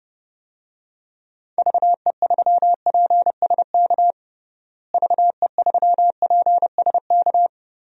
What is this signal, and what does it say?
Morse code, heard as a keyed tone, about 30 wpm, 700 Hz. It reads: VE3PSK VE3PSK
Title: DXCC-100 Call Signs - 6 Characters - Repeated 3x - 30wpm